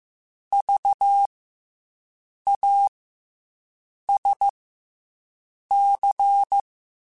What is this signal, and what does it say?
Morse code, heard as a keyed tone, about 15 wpm, 800 Hz. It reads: V A S C
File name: Special INGs / Special INGs-mp3s / SlowToFaster ing @13